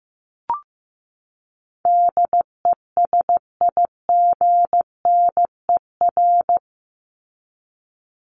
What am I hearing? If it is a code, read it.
Morse code, heard as a keyed tone, about 15 wpm, 700 Hz. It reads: DESIGNER
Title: Top 1501-1750 Words 15wpm